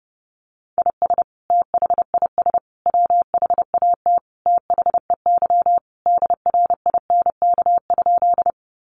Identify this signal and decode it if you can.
Morse code, heard as a keyed tone, about 30 wpm, 700 Hz. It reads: IS THIS WHAT THEY DRINK?
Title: Sentences from Top 1500 Words 30wpm